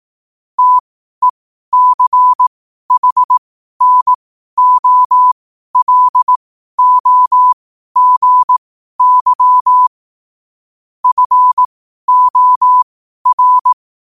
Morse code, heard as a keyed tone, about 18 wpm, spaced as for 12 wpm, 1000 Hz. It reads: TECHNOLOGY FOR
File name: 12wpm